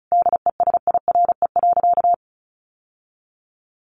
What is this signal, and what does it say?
Morse code, heard as a keyed tone, about 34 wpm, 700 Hz. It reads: DESIRE.
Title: Sentences from Top 1100 Words 35wpm